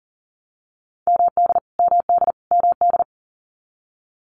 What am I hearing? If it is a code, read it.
Morse code, heard as a keyed tone, about 40 wpm, 700 Hz. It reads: MD MD MD
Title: US States Repeated 3x 40wpm